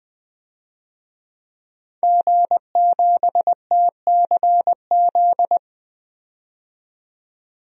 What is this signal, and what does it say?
Morse code, heard as a keyed tone, about 21 wpm, 700 Hz. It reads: G7TCZ
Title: DXCC-100 Call Signs - 5 Characters - 20wpm